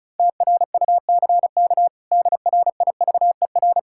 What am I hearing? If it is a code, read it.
Morse code, heard as a keyed tone, about 36 wpm, 700 Hz. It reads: TRUCK DRIVER
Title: K1USN SST QSO Practice 35wpm